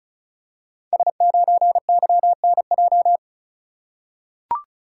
Morse code, heard as a keyed tone, about 36 wpm, 690 Hz. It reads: S9YNJ
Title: Code Groups - 5 Letter:Numbers - 35wpm